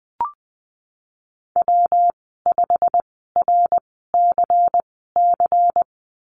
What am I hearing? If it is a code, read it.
Morse code, heard as a keyed tone, about 20 wpm, spaced as for 15 wpm, 700 Hz. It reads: W5RCC
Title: 122 Lesson 20g - Number 3 - Call Signs 20wpm